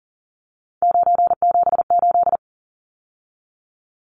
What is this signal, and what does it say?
Morse code, heard as a keyed tone, about 40 wpm, 700 Hz. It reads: 978
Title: Numbers - 3-Digit 40wpm